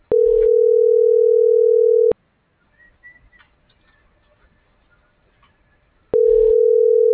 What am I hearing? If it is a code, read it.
Ringback tone (North America), 440 + 480 Hz.